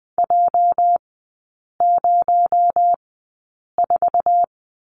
Morse code, heard as a keyed tone, about 20 wpm, spaced as for 8 wpm, 700 Hz. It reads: J04
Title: 218 Lesson 35e - Number 0 - Three Characters 20wpm